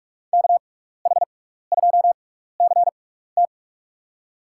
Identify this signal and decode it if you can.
Morse code, heard as a keyed tone, about 44 wpm, spaced as for 15 wpm, 700 Hz. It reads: KH2CT
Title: USA Call Signs - 5 Characters - 6x Character Spacing - 45wpm